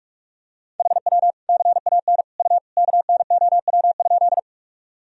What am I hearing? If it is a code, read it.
Morse code, heard as a keyed tone, about 43 wpm, 700 Hz. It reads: HW CAN U KNOW?
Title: Sentences from Top 100 Words and Abbreviations 45wpm